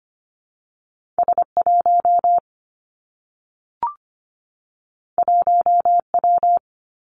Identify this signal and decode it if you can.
Morse code, heard as a keyed tone, about 25 wpm, 700 Hz. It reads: S1 1W
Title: Two Challenging Letter-Number Combinations 25wpm